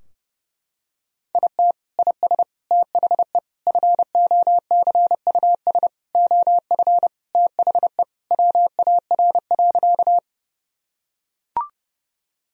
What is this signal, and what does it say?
Morse code, heard as a keyed tone, about 30 wpm, 700 Hz. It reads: IT IS THE FOCUS OF THE WAR.
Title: Sentences from Top 1000 Words - Review Part 4 - 30wpm